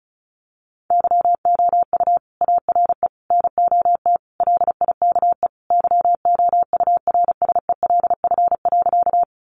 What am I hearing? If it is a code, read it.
Morse code, heard as a keyed tone, about 34 wpm, 700 Hz. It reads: YOU ARE NOT LIKE YOURSELF.
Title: Sentences from Top 600 Words 35wpm